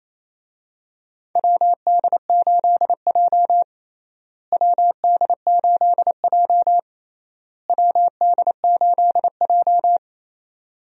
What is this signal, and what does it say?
Morse code, heard as a keyed tone, about 28 wpm, 700 Hz. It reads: WD8J WD8J WD8J
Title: USA Call Signs - 4 Characters - Repeated 3x - 28wpm